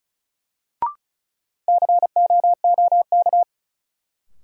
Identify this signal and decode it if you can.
Morse code, heard as a keyed tone, about 34 wpm, 700 Hz. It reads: COOK
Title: Sentences from Top 700 Words - New Word Drill - 35wpm